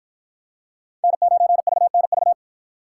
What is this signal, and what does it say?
Morse code, heard as a keyed tone, about 53 wpm, 700 Hz. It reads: N9VNV